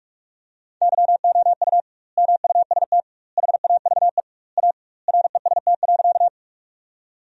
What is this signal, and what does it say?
Morse code, heard as a keyed tone, about 41 wpm, 700 Hz. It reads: YOU MUST HAVE A REST.
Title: 2x Word-Spacing - Sentences from Top 500 Words - 45wpm